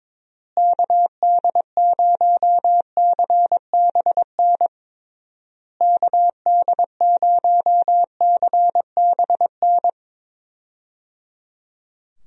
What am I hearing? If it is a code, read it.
Morse code, heard as a keyed tone, about 22 wpm, 700 Hz. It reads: KD0CBN KD0CBN
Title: USA Call Signs - 6 Characters - Repeated 3x - 22wpm